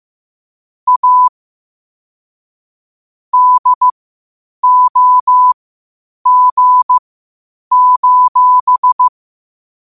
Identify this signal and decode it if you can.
Morse code, heard as a keyed tone, about 15 wpm, spaced as for 8 wpm, 1000 Hz. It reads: A DOG:
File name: qotdfiles / 8wpm